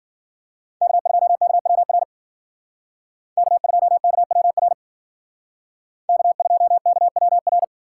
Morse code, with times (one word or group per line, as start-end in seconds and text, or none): 0.8-2.0 K2KWR
3.4-4.7 K2KWR
6.1-7.6 K2KWR